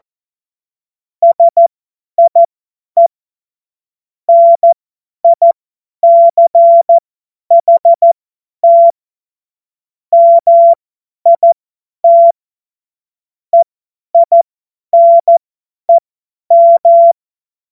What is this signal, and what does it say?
Morse code, heard as a keyed tone, about 14 wpm, spaced as for 10 wpm, 680 Hz. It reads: SIE NICHT MIT EINEM
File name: Morsekurs / Witze / Witz 09. Februar 2021, 50 F70